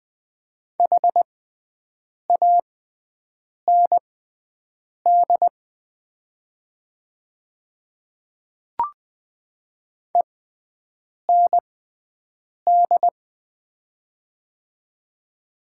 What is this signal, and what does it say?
Morse code, heard as a keyed tone, about 20 wpm, spaced as for 7 wpm, 700 Hz. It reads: HAND END
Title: 6x Character Spacing - Top 200 Words - 20wpm